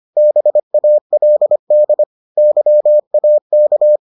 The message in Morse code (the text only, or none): BALD YAK